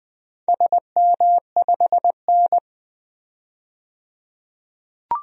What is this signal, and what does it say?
Morse code, heard as a keyed tone, about 20 wpm, 700 Hz. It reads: SM5N
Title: DXCC-100 Call Signs - 4 Characters - 20wpm